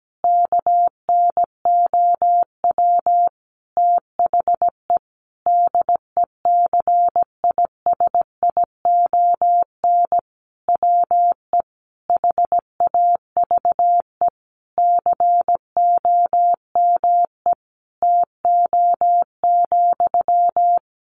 Morse code, heard as a keyed tone, about 17 wpm, 700 Hz. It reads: KNOW THE DECISION WE HAVE COME TO,
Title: Sentences from Top 1000 Words 17wpm